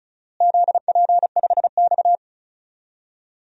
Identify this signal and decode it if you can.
Morse code, heard as a keyed tone, about 36 wpm, 700 Hz. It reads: ZP5X
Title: DXCC-100 Call Signs - 4 Characters - Repeated 2x - 35wpm